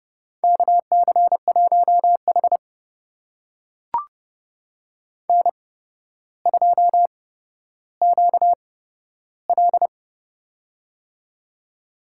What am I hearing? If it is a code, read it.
Morse code, heard as a keyed tone, about 30 wpm, spaced as for 13 wpm, 700 Hz. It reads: KC1H N2QL
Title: USA Call Signs - 4 Characters - 8x Character Spacing - 30wpm